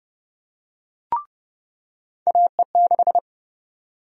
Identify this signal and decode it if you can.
Morse code, heard as a keyed tone, about 30 wpm, 700 Hz. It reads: AE6
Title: USA Call Sign Prefixes - 30wpm